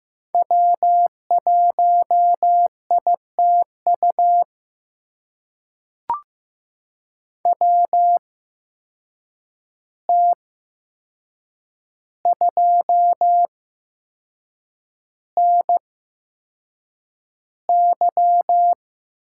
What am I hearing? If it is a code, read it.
Morse code, heard as a keyed tone, about 15 wpm, spaced as for 6 wpm, 700 Hz. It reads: W1ITU WT2NY
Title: USA Call Signs - 5 Characters - 8x Character Spacing - 15wpm